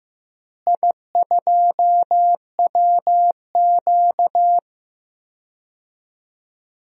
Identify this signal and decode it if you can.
Morse code, heard as a keyed tone, about 15 wpm, 700 Hz. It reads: I2WQ
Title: DXCC-100 Call Signs - 4 Characters - Repeated 2x - 15wpm